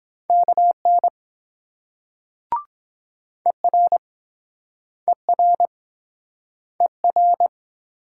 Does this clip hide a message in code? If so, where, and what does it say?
Yes: Morse code from 0.3 to 7.5 s, heard as a keyed tone, about 23 wpm, 700 Hz. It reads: KN ER ER ER